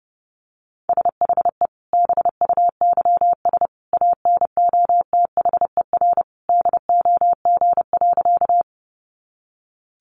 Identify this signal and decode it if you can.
Morse code, heard as a keyed tone, about 30 wpm, 700 Hz. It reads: SHE BUYS ANOTHER DOG.
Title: Easy Phrases 30wpm